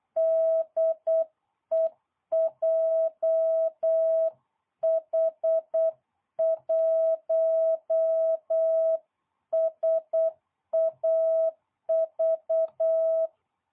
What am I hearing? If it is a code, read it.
Morse code, heard as a keyed tone, about 8 wpm, 660 Hz. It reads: DE JH1SAV